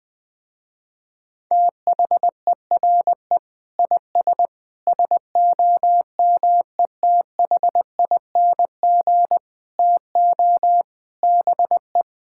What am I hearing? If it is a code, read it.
Morse code, heard as a keyed tone, about 20 wpm, 700 Hz. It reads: THERE IS SOMETHING TO BE
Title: Sentences with 7-Word Phrases - Spoken Beginning - 20wpm